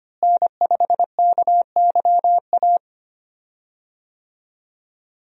Morse code, heard as a keyed tone, about 25 wpm, 700 Hz. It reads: N5KYA
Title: USA Call Signs - 5 Characters - Repeated 2x - 25wpm